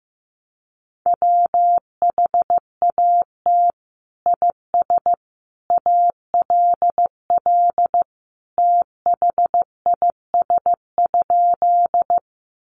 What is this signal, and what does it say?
Morse code, heard as a keyed tone, about 15 wpm, 700 Hz. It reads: WHAT IS ALL THIS?
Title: Sentences from Top 100 Words 15wpm